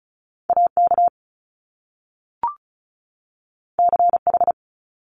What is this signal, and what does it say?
Morse code, heard as a keyed tone, about 36 wpm, 700 Hz. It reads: AK CH
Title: Top Two Letter Combinations in Words 35wpm